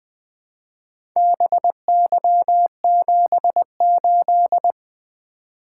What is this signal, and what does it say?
Morse code, heard as a keyed tone, about 20 wpm, 700 Hz. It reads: BY78
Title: Four Letter-Number Combinations - 20wpm